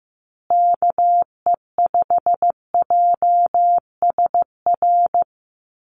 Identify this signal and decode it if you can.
Morse code, heard as a keyed tone, about 15 wpm, 700 Hz. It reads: KE5JSR
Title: USA Call Signs Repeated 3x 15wpm